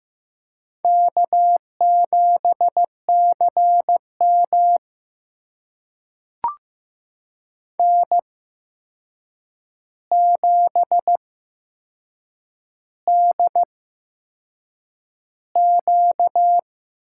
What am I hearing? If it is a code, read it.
Morse code, heard as a keyed tone, about 15 wpm, spaced as for 6 wpm, 700 Hz. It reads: K7CM N7DQ
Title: USA Call Signs - 4 Characters - 8x Character Spacing - 15wpm